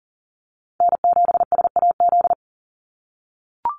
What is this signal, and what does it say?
Morse code, heard as a keyed tone, about 40 wpm, 700 Hz. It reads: N7SAZ
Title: USA Call Signs 40wpm